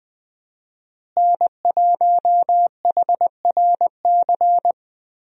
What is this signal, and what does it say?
Morse code, heard as a keyed tone, about 20 wpm, 700 Hz. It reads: N1HRC
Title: USA Call Signs - 5 Characters - Repeated 3x - 20wpm